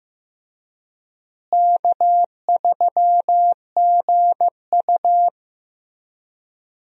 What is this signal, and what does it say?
Morse code, heard as a keyed tone, about 16 wpm, 700 Hz. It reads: K3GU
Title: USA Call Signs - 4 Characters - Repeated 3x - 15wpm